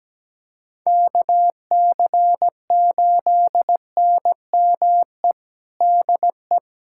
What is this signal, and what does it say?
Morse code, heard as a keyed tone, about 17 wpm, 700 Hz. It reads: KC8NME DE
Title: USA Callsign DE Callsign - 6 Characters - 17wpm